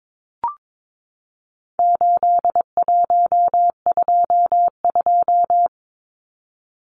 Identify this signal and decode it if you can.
Morse code, heard as a keyed tone, about 22 wpm, 700 Hz. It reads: 8122
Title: Numbers - 4-Digit 22wpm